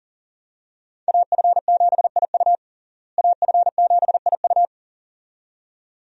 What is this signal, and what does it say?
Morse code, heard as a keyed tone, about 37 wpm, 700 Hz. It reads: AF7IU AF7IU
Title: USA Call Signs - 5 Characters - Repeated 2x - 40wpm